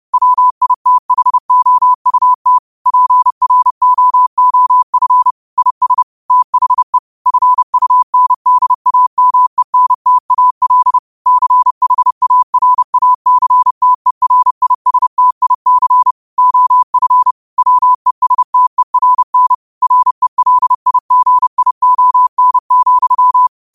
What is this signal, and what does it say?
Morse code, heard as a keyed tone, about 30 wpm, 1000 Hz. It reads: WITHOUT PROOF IS THE FUNDAMENTAL CHARACTERISTIC OF WESTERN RELIGION,